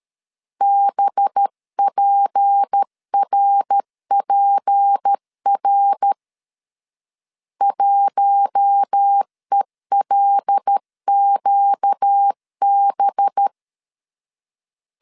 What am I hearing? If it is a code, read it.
Morse code, heard as a keyed tone, about 13 wpm, 800 Hz. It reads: BPRPR 1ELQB
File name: CW / DET / 1 6 1500C S C 12wpm 3533